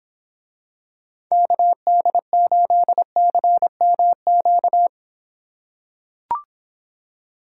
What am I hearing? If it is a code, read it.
Morse code, heard as a keyed tone, about 26 wpm, 700 Hz. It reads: KD8CMQ